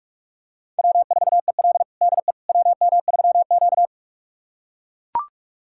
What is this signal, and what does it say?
Morse code, heard as a keyed tone, about 41 wpm, 700 Hz. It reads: W4EL DE WM3Q
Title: USA Callsign DE Callsign - 4 Characters - 45wpm